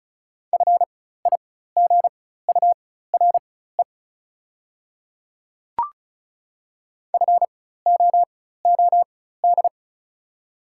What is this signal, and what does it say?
Morse code, heard as a keyed tone, about 34 wpm, spaced as for 16 wpm, 690 Hz. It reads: FIGURE FOOD
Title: 4x Character Spacing - Top 500 Words - 35wpm